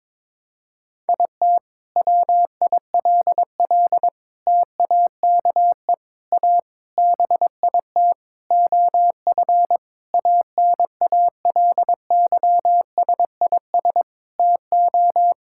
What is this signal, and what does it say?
Morse code, heard as a keyed tone, about 22 wpm, 700 Hz. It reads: IT WILL TAKE A BIT OF ANALYSIS TO
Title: Sentences from Top 1000 Words - Review Part 5 - 22wpm